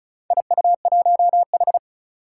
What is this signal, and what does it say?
Morse code, heard as a keyed tone, about 36 wpm, 700 Hz. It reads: IU1H
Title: Four Challenging Letter-Number Combinations 35wpm